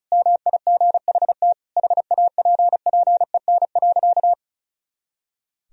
Morse code, heard as a keyed tone, about 34 wpm, 700 Hz. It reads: MIGHT HAPPEN.
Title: Sentences from Top 400 Words and Abbreviations 35wpm